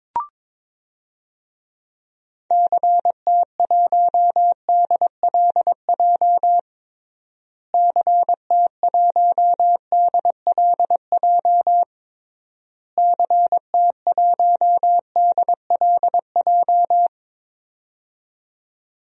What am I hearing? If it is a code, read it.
Morse code, heard as a keyed tone, about 22 wpm, 700 Hz. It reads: CT1DLJ CT1DLJ CT1DLJ
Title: DXCC-100 Call Signs - 6 Characters - Repeated 3x - 22wpm